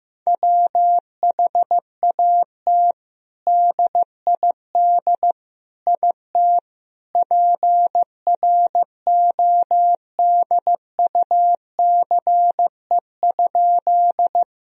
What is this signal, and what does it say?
Morse code, heard as a keyed tone, about 15 wpm, 700 Hz. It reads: WHAT DID IT PRODUCE?
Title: Sentences from Top 1000 Words - Review Part 3 - 15wpm